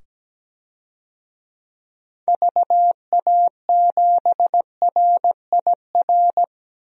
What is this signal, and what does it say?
Morse code, heard as a keyed tone, about 17 wpm, 700 Hz. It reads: VA7RIR
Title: DXCC-100 Call Signs - 6 Characters - 17wpm